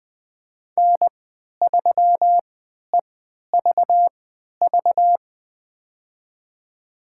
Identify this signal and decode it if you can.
Morse code, heard as a keyed tone, about 20 wpm, spaced as for 11 wpm, 700 Hz. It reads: N3EVV
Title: USA Call Signs - 5 Characters - 3x Character Spacing - 20wpm